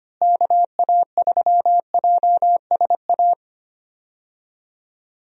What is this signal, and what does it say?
Morse code, heard as a keyed tone, about 25 wpm, 700 Hz. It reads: KA3JSA
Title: USA Call Signs - 6 Characters - Repeated 3x - 25wpm